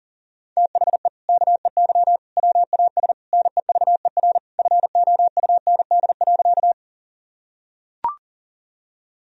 Morse code, heard as a keyed tone, about 40 wpm, 700 Hz. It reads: THE KEY WAS NEVER FOUND.
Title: Sentences from Top 1000 Words - Review Part 4 - 40wpm